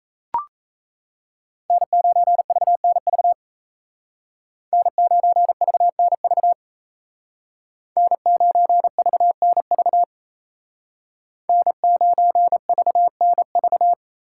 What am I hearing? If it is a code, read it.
Morse code, heard as a keyed tone, about 34 wpm, 700 Hz. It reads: N9VNV N9VNV N9VNV N9VNV
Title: Speed Racing - USA 1x3 Call Signs 28wpm